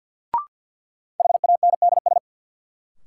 Morse code, heard as a keyed tone, about 52 wpm, 700 Hz. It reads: HANDS
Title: Sentences from Top 1000 Words - Review Part 6 - New Word Drill - 50wpm